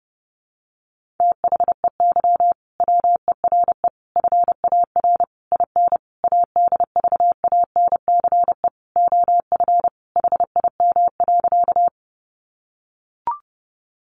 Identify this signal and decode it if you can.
Morse code, heard as a keyed tone, about 30 wpm, 700 Hz. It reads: THEY WERE FAR IN ADVANCE OF HIM.
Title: Sentences from Top 1400 Words 30wpm